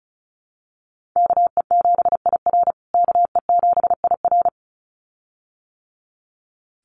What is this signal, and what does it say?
Morse code, heard as a keyed tone, about 34 wpm, 700 Hz. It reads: KE7IR KE7IR
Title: USA Call Signs Repeated 2x 35wpm